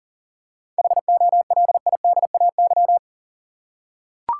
Morse code, heard as a keyed tone, about 40 wpm, 700 Hz. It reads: HOLIDAY
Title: Seven Letter Words - 40wpm